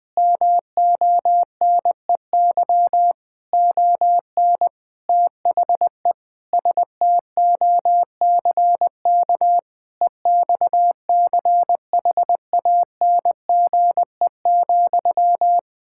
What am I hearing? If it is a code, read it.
Morse code, heard as a keyed tone, about 20 wpm, 700 Hz. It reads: MONEY ON THE STOCK EXCHANGE,
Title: Sentences from Top 1000 Words - Review Part 6 - 20wpm